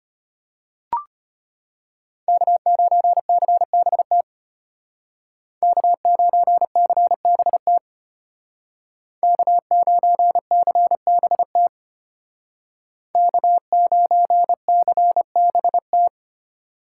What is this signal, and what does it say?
Morse code, heard as a keyed tone, about 32 wpm, 700 Hz. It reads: K9CBT K9CBT K9CBT K9CBT